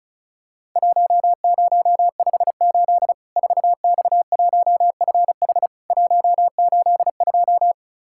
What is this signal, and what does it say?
Morse code, heard as a keyed tone, about 36 wpm, 700 Hz. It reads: 1058 4X1FH 182
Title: Worked All Europe DX Contest - QTC Exchange - 35wpm